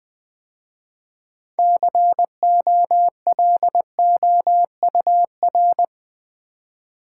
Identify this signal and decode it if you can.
Morse code, heard as a keyed tone, about 20 wpm, 700 Hz. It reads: COLOUR
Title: 6 Letter Words - First 3 Letters Given - 20wpm